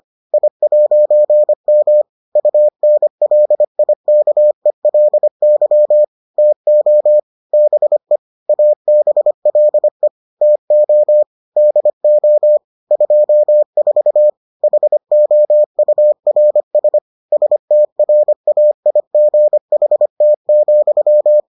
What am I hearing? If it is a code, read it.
Morse code, heard as a keyed tone, about 25 wpm, 600 Hz. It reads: I'M UNLIKELY TO BE ABLE TO DO 24 HOURS STRAIGHT,